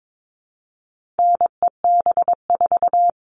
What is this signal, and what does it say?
Morse code, heard as a keyed tone, about 22 wpm, 700 Hz. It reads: NEB4
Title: All Cut Numbers - 4-Digit 22wpm